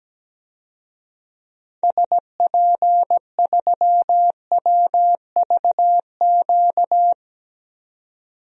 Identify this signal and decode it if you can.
Morse code, heard as a keyed tone, about 17 wpm, 700 Hz. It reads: SP3WVQ
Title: DXCC-100 Call Signs - 6 Characters - 17wpm